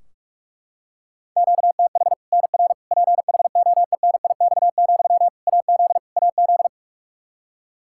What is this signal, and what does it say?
Morse code, heard as a keyed tone, about 43 wpm, 700 Hz. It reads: QTH NR PHOENIX, AZ AZ
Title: K1USN SST QSO Practice 45wpm